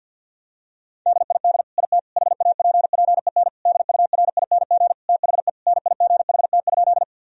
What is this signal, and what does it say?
Morse code, heard as a keyed tone, about 50 wpm, 700 Hz. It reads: DID IT HAPPEN DURING THE NIGHT?
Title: Sentences from Top 1000 Words - Review Part 2 - 50wpm